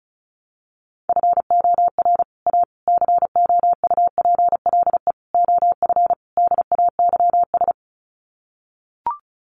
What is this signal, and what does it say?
Morse code, heard as a keyed tone, about 34 wpm, 700 Hz. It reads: FOR A COUPLE OF DAYS
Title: Sets of 5 Words Encore2 35wpm